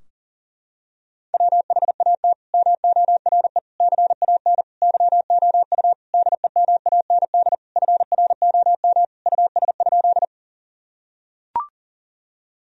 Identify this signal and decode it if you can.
Morse code, heard as a keyed tone, about 40 wpm, 700 Hz. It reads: WHAT MORE CAN YOU DEMAND FROM US?
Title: Sentences from Top 1000 Words - Review Part 4 - 40wpm